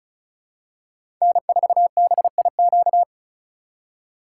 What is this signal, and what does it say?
Morse code, heard as a keyed tone, about 36 wpm, 700 Hz. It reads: N4BIQ
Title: USA Call Signs - 5 Characters - 6x Character Spacing - 35wpm